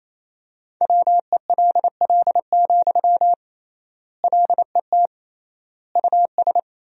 Morse code, heard as a keyed tone, about 28 wpm, 700 Hz. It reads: WELL, LET US
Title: 3x Word-Spacing - Sentences from Top 300 Words - 28wpm